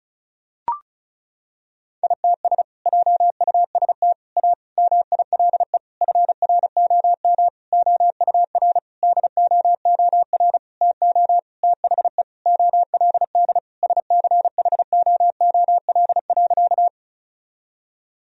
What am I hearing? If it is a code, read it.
Morse code, heard as a keyed tone, about 36 wpm, 700 Hz. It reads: ITS JUST A MILE FROM OUR DOOR TO THE OLD SCHOOL.
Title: Sentences from Top 1000 Words - Review Part 1 - 35wpm